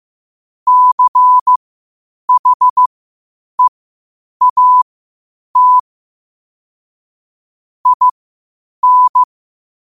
Morse code, heard as a keyed tone, about 15 wpm, spaced as for 8 wpm, 1000 Hz. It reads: CHEAT IN
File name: qotdfiles / 8wpm